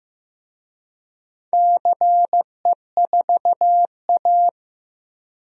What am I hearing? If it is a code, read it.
Morse code, heard as a keyed tone, about 15 wpm, 700 Hz. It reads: CE4A
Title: DXCC-100 Call Signs - 4 Characters - 15wpm